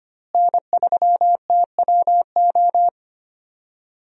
Morse code, heard as a keyed tone, about 25 wpm, 700 Hz. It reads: N3TWO
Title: USA Call Signs - 5 Characters - Repeated 2x - 25wpm